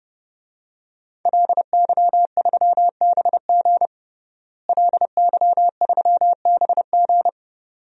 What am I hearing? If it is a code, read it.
Morse code, heard as a keyed tone, about 30 wpm, 700 Hz. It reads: LY3BG LY3BG
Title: DXCC-100 Call Signs - 5 Characters - Repeated 2x - 30wpm